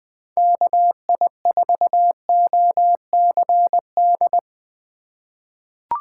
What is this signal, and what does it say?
Morse code, heard as a keyed tone, about 21 wpm, 700 Hz. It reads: KI4OCD
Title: USA Call Signs - 6 Characters - 4x Character Spacing - 20wpm